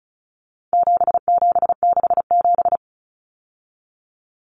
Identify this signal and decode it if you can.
Morse code, heard as a keyed tone, about 34 wpm, 700 Hz. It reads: 7767
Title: Numbers - 4-Digit 35wpm